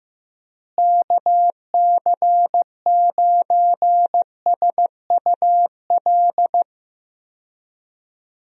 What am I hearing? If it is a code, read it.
Morse code, heard as a keyed tone, about 15 wpm, 700 Hz. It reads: KC9SUL
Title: USA Call Signs - 6 Characters - Repeated 2x - 15wpm